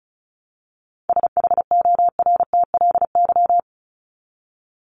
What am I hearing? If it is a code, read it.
Morse code, heard as a keyed tone, about 34 wpm, 700 Hz. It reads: SHORTLY
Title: LY Suffix - Top 100 Words with Base Words 35wpm